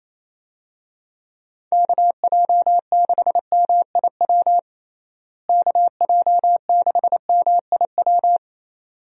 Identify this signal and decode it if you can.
Morse code, heard as a keyed tone, about 28 wpm, 700 Hz. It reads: KJ6MIW KJ6MIW
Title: USA Call Signs - 6 Characters - Repeated 2x - 28wpm